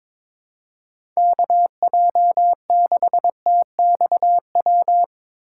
Morse code, heard as a keyed tone, about 22 wpm, 700 Hz. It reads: KJ6TXW
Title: USA Call Signs - 6 Characters - 2x Character Spacing - 22wpm